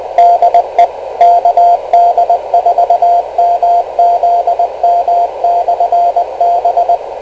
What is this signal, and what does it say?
Morse code, heard as a keyed tone, about 20 wpm, 700 Hz, with radio noise. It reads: DE KD4MZM/B
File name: kl7fdq / beacons / kd4mzm-rst546